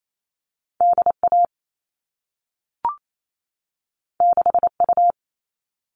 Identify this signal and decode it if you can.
Morse code, heard as a keyed tone, about 28 wpm, 700 Hz. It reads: DA 6U
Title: All Cut Numbers - 2-Digit 28wpm